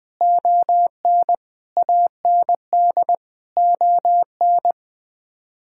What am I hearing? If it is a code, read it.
Morse code, heard as a keyed tone, about 20 wpm, 700 Hz. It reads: ON AND ON
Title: Sets of 6 Words - 20wpm